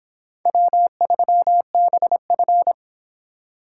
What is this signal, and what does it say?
Morse code, heard as a keyed tone, about 26 wpm, 700 Hz. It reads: W3BF